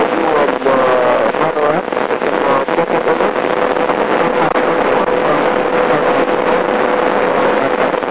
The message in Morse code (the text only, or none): none